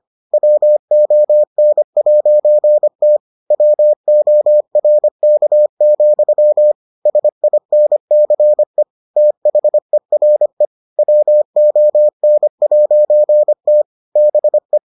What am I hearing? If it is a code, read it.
Morse code, heard as a keyed tone, about 25 wpm, 600 Hz. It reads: WON'T WORK, SINCE THERE WON'T BE